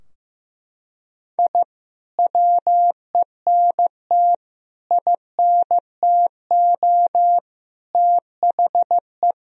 Morse code, heard as a keyed tone, about 15 wpm, 700 Hz. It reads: I WENT INTO THE
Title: Sentences from Top 1000 Words - Review Part 4 - 15wpm